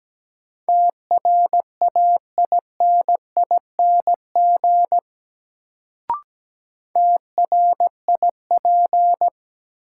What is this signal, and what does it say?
Morse code, heard as a keyed tone, about 17 wpm, 700 Hz. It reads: TRAINING TRIP
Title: Sentences from Top 1000 Words - Review Part 4 - New Word Drill - 17wpm